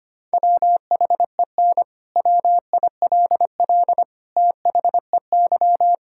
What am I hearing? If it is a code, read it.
Morse code, heard as a keyed tone, about 25 wpm, 700 Hz. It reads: WHEN WILL THEY
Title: Sentences from Top 1000 Words - Review Part 6 - 25wpm